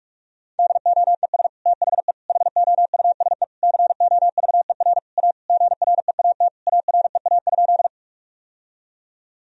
Morse code, heard as a keyed tone, about 42 wpm, 700 Hz. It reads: DOES THE HOUSE COVER A GREAT AREA?
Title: Sentences from Top 1000 Words - Review Part 2 - 45wpm